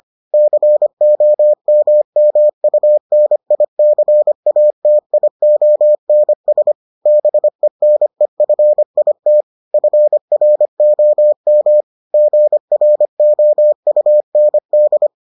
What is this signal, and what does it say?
Morse code, heard as a keyed tone, about 25 wpm, 600 Hz. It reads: COMMUNICATIONS BENEFIT FROM GROUND